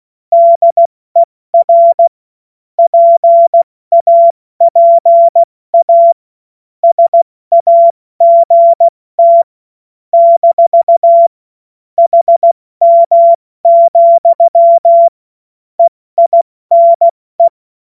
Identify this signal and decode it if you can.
Morse code, heard as a keyed tone, about 16 wpm, 680 Hz. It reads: DER PAPA SAGT - HM, EINE
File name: Morsekurs / Witze / Witz 14. Februar 2023 70 F80